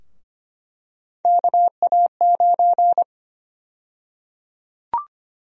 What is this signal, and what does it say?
Morse code, heard as a keyed tone, about 25 wpm, 700 Hz. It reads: KA9
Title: USA Call Sign Prefixes - Repeated 2x - 25wpm